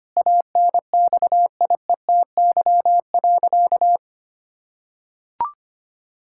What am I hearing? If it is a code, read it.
Morse code, heard as a keyed tone, about 25 wpm, 700 Hz. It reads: ANXIETY.
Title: Sentences from Top 1600 Words 25wpm